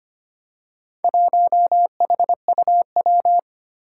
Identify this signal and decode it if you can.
Morse code, heard as a keyed tone, about 25 wpm, 700 Hz. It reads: 1HUW
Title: Four Challenging Letter-Number Combinations 25wpm